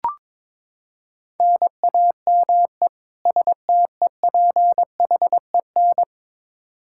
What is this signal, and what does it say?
Morse code, heard as a keyed tone, about 22 wpm, 700 Hz. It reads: NAME STEPHEN
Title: Practice QSO - Exchange 1 - Beginning - 22wpm